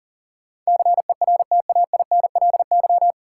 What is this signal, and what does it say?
Morse code, heard as a keyed tone, about 37 wpm, 700 Hz. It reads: CERTAINLY
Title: Sentences from Top 900 Words - New Word Drill - 40wpm